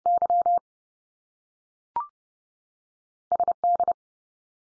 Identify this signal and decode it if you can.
Morse code, heard as a keyed tone, about 30 wpm, 700 Hz. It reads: Y SD
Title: US States 30WPM